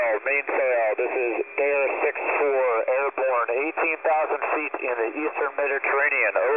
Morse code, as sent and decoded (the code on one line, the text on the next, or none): none